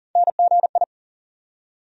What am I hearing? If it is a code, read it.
Morse code, heard as a keyed tone, about 40 wpm, 700 Hz. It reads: NGI
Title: USA Call Sign Postfixes - 3 Letters 40wpm